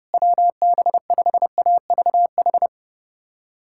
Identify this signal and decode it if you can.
Morse code, heard as a keyed tone, about 32 wpm, 700 Hz. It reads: WB5AVH
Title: USA Callsign DE Callsign - 6 Characters - 30wpm